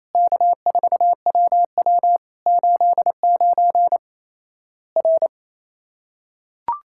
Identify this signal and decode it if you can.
Morse code, heard as a keyed tone, about 28 wpm, 700 Hz. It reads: K4WW 89 R
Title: Worked All Europe DX Contest - QTC Exchange - 28wpm